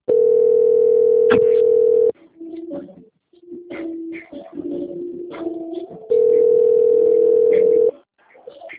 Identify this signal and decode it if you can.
Ringback tone (North America), 440 + 480 Hz.
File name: out- - - - - .50834